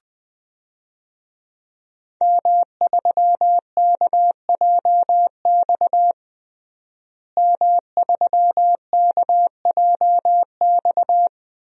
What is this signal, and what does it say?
Morse code, heard as a keyed tone, about 20 wpm, 700 Hz. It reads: M3KJX M3KJX
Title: DXCC-100 Call Signs - 5 Characters - Repeated 3x - 20wpm